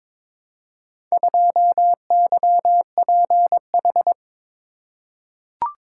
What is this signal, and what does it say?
Morse code, heard as a keyed tone, about 22 wpm, 700 Hz. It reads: 2YPH
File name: practice / 22wpm / Four Letter-Number Combinations - 22wpm